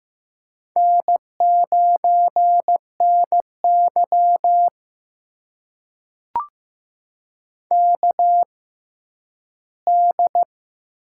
Morse code, heard as a keyed tone, about 15 wpm, spaced as for 10 wpm, 700 Hz. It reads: N9NY KD
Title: USA Call Signs - 4 Characters - 6x Character Spacing - 15wpm